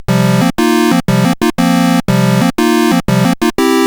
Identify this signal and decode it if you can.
There is no code.